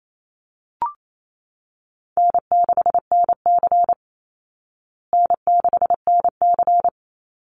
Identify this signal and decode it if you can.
Morse code, heard as a keyed tone, about 28 wpm, 700 Hz. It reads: N6NC N6NC
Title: Call Signs Repeated 3x - 28wpm